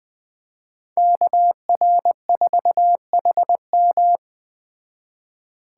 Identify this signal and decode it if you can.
Morse code, heard as a keyed tone, about 20 wpm, 700 Hz. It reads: KR4HM
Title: Speed Racing - USA 2x2 Call Signs 17wpm